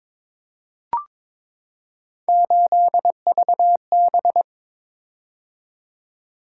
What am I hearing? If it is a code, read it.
Morse code, heard as a keyed tone, about 22 wpm, 700 Hz. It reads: 8VB
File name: practice / 22wpm / Three Letter-Number Combinations - 22wpm